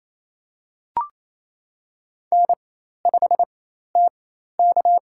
Morse code, heard as a keyed tone, about 27 wpm, spaced as for 13 wpm, 700 Hz. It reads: N5TK
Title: USA Call Signs - 4 Characters - 4x Character Spacing - 28wpm